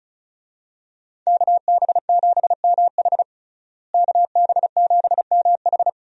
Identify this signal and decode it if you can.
Morse code, heard as a keyed tone, about 36 wpm, 700 Hz. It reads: KB7MH KB7MH
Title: USA Call Signs - 5 Characters - Repeated 3x - 35wpm